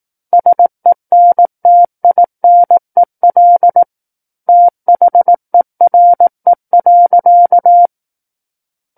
Morse code, heard as a keyed tone, about 18 wpm, 700 Hz. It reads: SENTINEL THERE.
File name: Book One / 09-Chapter 9 - 18 WPM